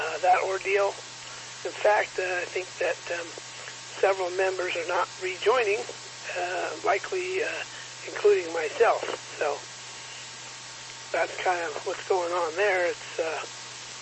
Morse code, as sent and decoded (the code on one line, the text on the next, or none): none